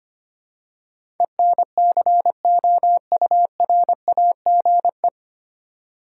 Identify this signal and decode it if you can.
Morse code, heard as a keyed tone, about 25 wpm, 700 Hz. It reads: ENCOURAGE
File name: Nine Letter Words - 25wpm